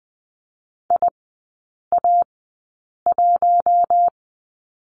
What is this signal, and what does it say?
Morse code, heard as a keyed tone, about 20 wpm, spaced as for 8 wpm, 700 Hz. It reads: IA1
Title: 047 Lesson 9e - Number 4 - Three Characters 20wpm